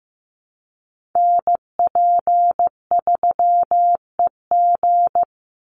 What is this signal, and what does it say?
Morse code, heard as a keyed tone, about 15 wpm, 700 Hz. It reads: NP3EG
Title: USA Call Signs 15wpm